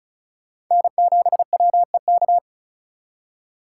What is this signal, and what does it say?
Morse code, heard as a keyed tone, about 48 wpm, spaced as for 42 wpm, 700 Hz. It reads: N7WEK
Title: Speed Racing - USA 1x3 Call Signs 35wpm